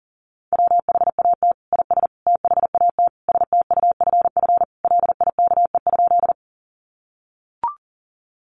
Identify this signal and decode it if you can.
Morse code, heard as a keyed tone, about 40 wpm, 700 Hz. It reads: WHAT IS THAT STUFF LIKE?
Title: Sentences from Top 1400 Words 40wpm